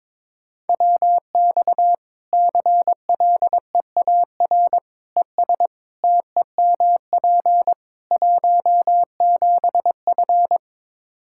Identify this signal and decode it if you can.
Morse code, heard as a keyed tone, about 22 wpm, 700 Hz. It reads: WX CLEAR ES TEMP 17F
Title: Practice QSO - Exchange 2 - Info - 22wpm